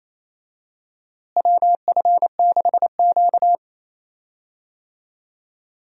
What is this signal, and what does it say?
Morse code, heard as a keyed tone, about 28 wpm, 700 Hz. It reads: WF6Q
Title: USA Call Signs - 4 Characters - Repeated 3x - 28wpm